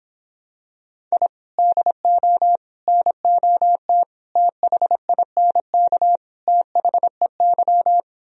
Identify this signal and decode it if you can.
Morse code, heard as a keyed tone, about 26 wpm, 700 Hz. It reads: I DO NOT THINK THEY